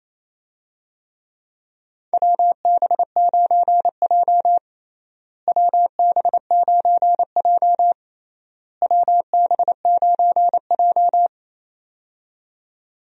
Morse code, heard as a keyed tone, about 28 wpm, 700 Hz. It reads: WB9J WB9J WB9J
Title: USA Call Signs - 4 Characters - Repeated 3x - 28wpm